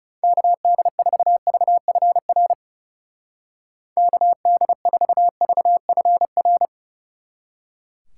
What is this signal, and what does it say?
Morse code, heard as a keyed tone, about 38 wpm, 700 Hz. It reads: KD4VFR KD4VFR